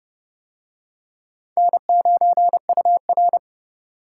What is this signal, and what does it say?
Morse code, heard as a keyed tone, about 27 wpm, 700 Hz. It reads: N9UR